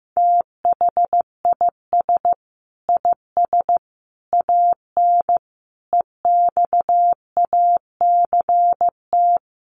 Morse code, heard as a keyed tone, about 15 wpm, 700 Hz. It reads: THIS IS AN EXACT
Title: Sentences from Top 1300 Words 15wpm